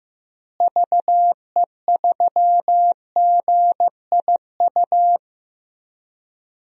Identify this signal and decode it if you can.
Morse code, heard as a keyed tone, about 15 wpm, 700 Hz. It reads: VE3GIU
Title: DXCC-100 Call Signs - 6 Characters - Repeated 3x - 15wpm